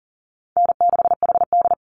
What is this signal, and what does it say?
Morse code, heard as a keyed tone, about 40 wpm, 700 Hz. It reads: N6HD
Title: Call Signs Repeated 3x - 40wpm